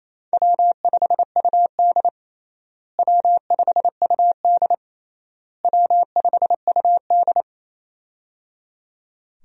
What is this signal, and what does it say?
Morse code, heard as a keyed tone, about 27 wpm, 700 Hz. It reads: W5UD W5UD W5UD
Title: USA Call Signs - 4 Characters - Repeated 3x - 28wpm